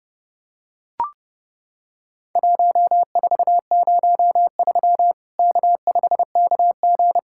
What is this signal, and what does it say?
Morse code, heard as a keyed tone, about 30 wpm, 700 Hz. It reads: 1403 K5KG
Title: Worked All Europe DX Contest - QTC Exchange - 30wpm